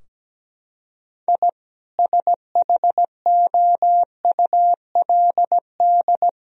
Morse code, heard as a keyed tone, about 17 wpm, 700 Hz. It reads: I SHOULD
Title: Sentences from Top 1000 Words - Review Part 3 - 17wpm